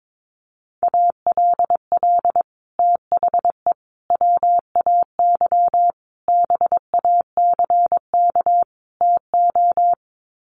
Morse code, heard as a keyed tone, about 22 wpm, 700 Hz. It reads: ALL THE WAY BACK TO
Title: Sets of 5 Words Encore 22wpm